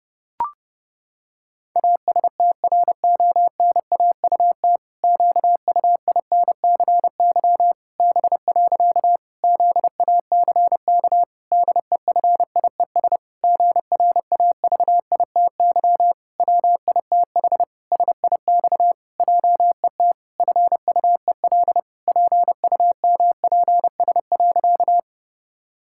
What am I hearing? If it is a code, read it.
Morse code, heard as a keyed tone, about 30 wpm, 700 Hz. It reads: ASTRONAUT QUINCY B. ZACK DEFIES GRAVITY WITH SIX JET FUEL PUMPS.
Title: Pangram Challenge v2 - 30wpm